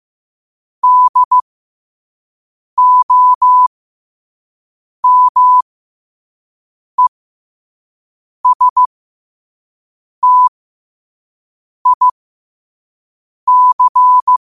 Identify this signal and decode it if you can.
Morse code, heard as a keyed tone, about 15 wpm, spaced as for 5 wpm, 1000 Hz. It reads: DOMESTIC
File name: qotdfiles / 5wpm